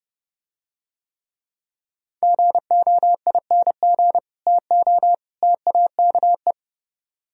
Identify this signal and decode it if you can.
Morse code, heard as a keyed tone, about 29 wpm, 700 Hz. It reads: GOING TO TAKE